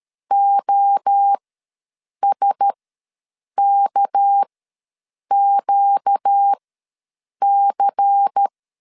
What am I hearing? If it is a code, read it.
Morse code, heard as a keyed tone, about 13 wpm, spaced as for 7 wpm, 800 Hz. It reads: OSKQC
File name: LEZIONI / 9 286 ws 12wpm 3351